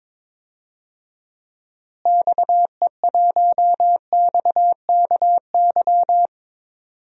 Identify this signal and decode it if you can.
Morse code, heard as a keyed tone, about 22 wpm, 700 Hz. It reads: XE1XKY
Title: DXCC-100 Call Signs - 6 Characters - Repeated 3x - 22wpm